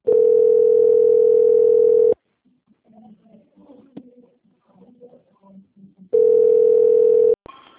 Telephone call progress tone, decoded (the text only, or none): ringback tone